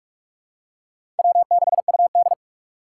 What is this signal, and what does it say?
Morse code, heard as a keyed tone, about 45 wpm, 700 Hz. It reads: W6UD
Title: USA Call Signs - 4 Characters - 2x Character Spacing - 45wpm